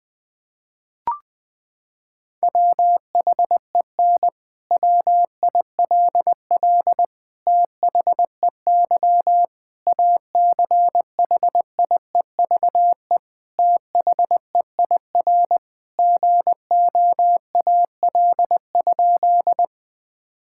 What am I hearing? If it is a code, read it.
Morse code, heard as a keyed tone, about 20 wpm, 700 Hz. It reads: WHEN WILL THEY ACHIEVE THEIR GOAL?
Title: Sentences from Top 1000 Words - Review Part 6 - 20wpm